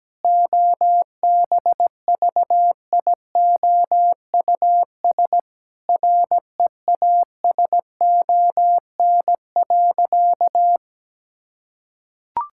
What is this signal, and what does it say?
Morse code, heard as a keyed tone, about 17 wpm, 700 Hz. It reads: OBVIOUS REASON.
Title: Sentences from Top 1600 Words 17wpm